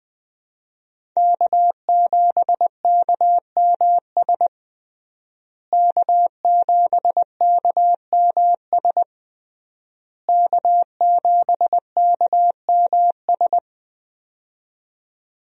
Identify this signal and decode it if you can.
Morse code, heard as a keyed tone, about 21 wpm, 700 Hz. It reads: K7KMS K7KMS K7KMS
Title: USA Call Signs - 5 Characters - Repeated 3x - 20wpm